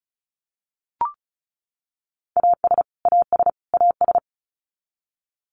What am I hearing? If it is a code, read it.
Morse code, heard as a keyed tone, about 34 wpm, 700 Hz. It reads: AS AS AS
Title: Top 500 Words Repeated 35wpm